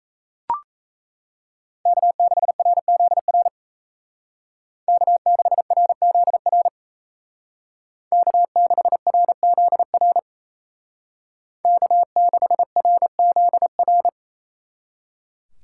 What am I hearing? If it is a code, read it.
Morse code, heard as a keyed tone, about 35 wpm, 700 Hz. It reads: K6RZR K6RZR K6RZR K6RZR
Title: Speed Racing - USA 1x3 Call Signs 28wpm